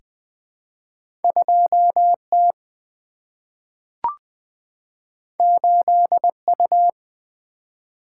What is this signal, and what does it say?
Morse code, heard as a keyed tone, about 20 wpm, 700 Hz. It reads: 2 T 8 U
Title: Single-Letter Single-Number 20wpm